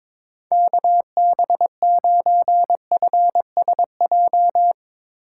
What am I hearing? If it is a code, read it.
Morse code, heard as a keyed tone, about 22 wpm, 700 Hz. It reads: KB9FSJ
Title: Speed Racing - USA 2x3 Call Signs 22wpm